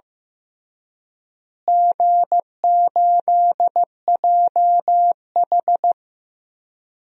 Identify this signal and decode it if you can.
Morse code, heard as a keyed tone, about 15 wpm, 700 Hz. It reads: G8JH
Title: DXCC-100 Call Signs - 4 Characters - Repeated 3x - 15wpm